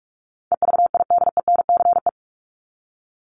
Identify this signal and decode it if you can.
Morse code, heard as a keyed tone, about 44 wpm, 700 Hz. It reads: EVIDENCE
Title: Top 1001-1250 Words 45wpm